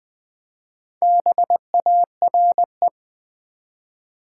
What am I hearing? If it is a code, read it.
Morse code, heard as a keyed tone, about 20 wpm, 700 Hz. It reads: BARE
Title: Build A Word - 2 Letters At A Time - 6-Letter Words - 20wpm